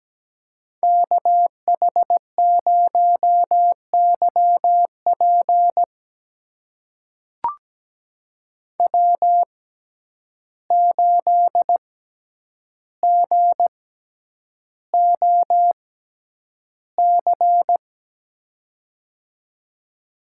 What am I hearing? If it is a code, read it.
Morse code, heard as a keyed tone, about 17 wpm, spaced as for 9 wpm, 700 Hz. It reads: KH0YP W8GOC
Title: USA Call Signs - 5 Characters - 6x Character Spacing - 17wpm